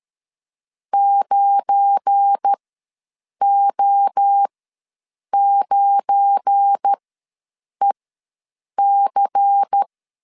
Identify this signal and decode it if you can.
Morse code, heard as a keyed tone, about 13 wpm, spaced as for 7 wpm, 800 Hz. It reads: 9O9EC
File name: LEZIONI / 7 GF9Z / 7 GF9 ws 12wpm 3344